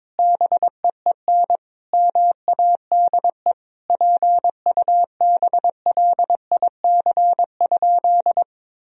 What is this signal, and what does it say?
Morse code, heard as a keyed tone, about 22 wpm, 700 Hz. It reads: BEEN MADE PUBLIC?
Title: Sentences from Top 1000 Words - Review Part 3 - 22wpm